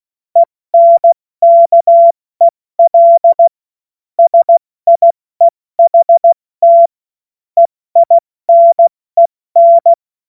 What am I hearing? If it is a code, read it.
Morse code, heard as a keyed tone, about 16 wpm, 680 Hz. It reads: ENKEL SIEHT EINEN
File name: Morsekurs / Witze / Witz 17. August 2021, 70 F80